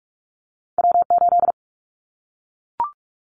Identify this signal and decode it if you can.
Morse code, heard as a keyed tone, about 44 wpm, 700 Hz. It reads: W8
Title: Callsign Prefixes for Top 25 DXCC - Repeated 3x - 45wpm